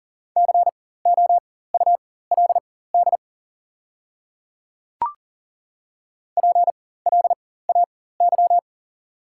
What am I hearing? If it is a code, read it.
Morse code, heard as a keyed tone, about 40 wpm, spaced as for 19 wpm, 690 Hz. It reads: COULD PLAY
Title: 4x Character Spacing - Top 200 Words - 40wpm